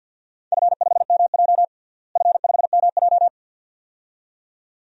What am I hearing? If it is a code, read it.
Morse code, heard as a keyed tone, about 51 wpm, 700 Hz. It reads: F5MJ F5MJ
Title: DXCC-100 Call Signs - 4 Characters - Repeated 2x - 50wpm